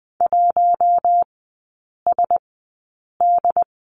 Morse code, heard as a keyed tone, about 20 wpm, spaced as for 8 wpm, 700 Hz. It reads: 1SD
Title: 074 Lesson 13e - Letter L - Three Characters 20wpm